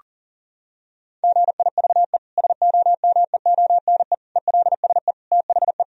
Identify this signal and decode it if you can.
Morse code, heard as a keyed tone, about 40 wpm, 700 Hz. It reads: GIVE SOMEONE ELSE THE
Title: Sentences from Top 1000 Words - Review Part 4 - 40wpm